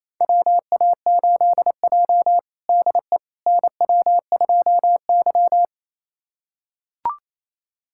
Morse code, heard as a keyed tone, about 28 wpm, 700 Hz. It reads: WA8J DE NW2Y
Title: USA Callsign DE Callsign - 4 Characters - 28wpm